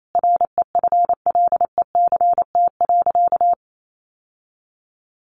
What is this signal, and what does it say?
Morse code, heard as a keyed tone, about 28 wpm, 700 Hz. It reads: REFLECT.
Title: Sentences from Top 1500 Words 28wpm